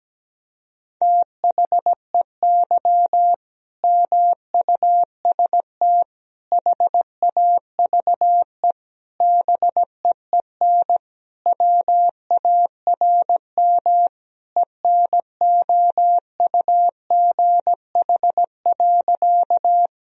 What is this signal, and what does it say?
Morse code, heard as a keyed tone, about 17 wpm, 700 Hz. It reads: THEY MUST HAVE BEEN WARM ENOUGH.
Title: Sentences from Top 1000 Words - Review Part 2 - 17wpm